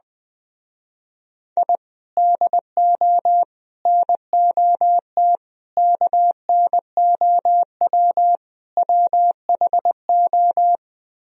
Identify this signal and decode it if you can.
Morse code, heard as a keyed tone, about 20 wpm, 700 Hz. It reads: I DO NOT KNOW WHO
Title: Speed-Racing - Sets of 5 Words 20wpm